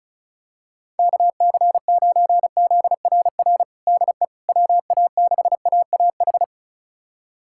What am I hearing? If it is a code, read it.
Morse code, heard as a keyed tone, about 34 wpm, 700 Hz. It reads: KC9ZRR DE WA6AAH
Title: USA Callsign DE Callsign 35wpm